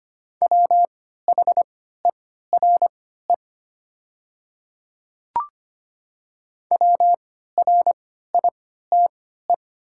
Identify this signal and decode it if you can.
Morse code, heard as a keyed tone, about 25 wpm, spaced as for 14 wpm, 700 Hz. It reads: WHERE WRITE
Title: 3x Character Spacing - Top 300 Words - 25wpm